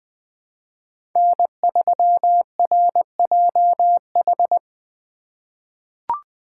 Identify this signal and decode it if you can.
Morse code, heard as a keyed tone, about 20 wpm, 700 Hz. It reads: N3RJH
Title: USA Call Signs - 5 Characters - 2x Character Spacing - 20wpm